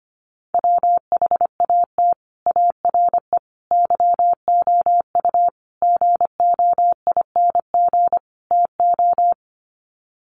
Morse code, heard as a keyed tone, about 25 wpm, 700 Hz. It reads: WHAT ARE YOU GOING TO
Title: Sets of 5 Words 25wpm